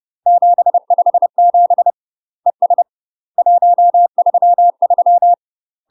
Morse code, heard as a keyed tone, about 30 wpm, 700 Hz. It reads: 757 ES 133
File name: UK MASTER QSOs TRAINING / 30WPM / UK MASTER QSOs 0019